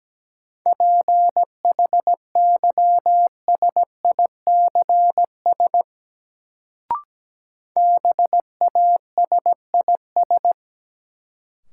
Morse code, heard as a keyed tone, about 17 wpm, 700 Hz. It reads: PHYSICS BASIS
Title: Sentences from Top 1000 Words - Review Part 5 - New Word Drill - 17wpm